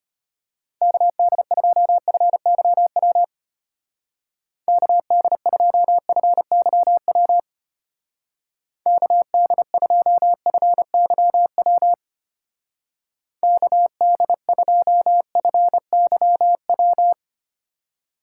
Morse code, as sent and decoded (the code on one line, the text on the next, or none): -.- -.. ..--- ..-. -.-- .-- / -.- -.. ..--- ..-. -.-- .-- / -.- -.. ..--- ..-. -.-- .-- / -.- -.. ..--- ..-. -.-- .--
KD2FYW KD2FYW KD2FYW KD2FYW